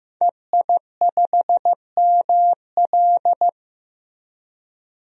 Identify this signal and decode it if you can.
Morse code, heard as a keyed tone, about 16 wpm, 700 Hz. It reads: EI5ML
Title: DXCC-100 Call Signs - 5 Characters - 15wpm